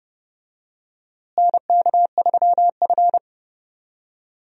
Morse code, heard as a keyed tone, about 30 wpm, 700 Hz. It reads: NK3F
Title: USA Call Signs - 4 Characters - Repeated 3x - 30wpm